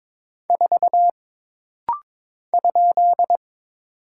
Morse code, heard as a keyed tone, about 22 wpm, 700 Hz. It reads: 4 ?
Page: This rapid-fire timing is geared to push your ICR (Instant Character Recognition) skills.